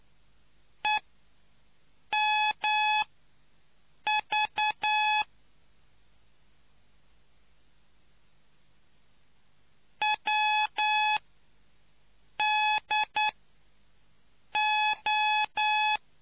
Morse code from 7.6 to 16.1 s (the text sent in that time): WDO